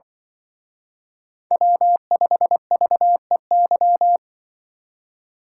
Morse code, heard as a keyed tone, about 24 wpm, 700 Hz. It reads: W5VEY
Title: Speed Racing - USA 1x3 Call Signs 20wpm